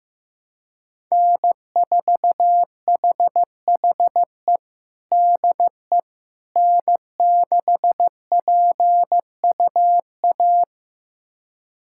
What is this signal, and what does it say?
Morse code, heard as a keyed tone, about 16 wpm, 700 Hz. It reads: N4HHE DE N6PUA
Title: USA Callsign DE Callsign - 5 Characters - 15wpm